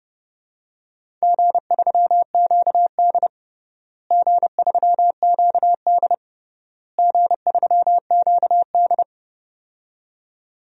Morse code, heard as a keyed tone, about 30 wpm, 700 Hz. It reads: G3QD G3QD G3QD
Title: DXCC-100 Call Signs - 4 Characters - Repeated 3x - 30wpm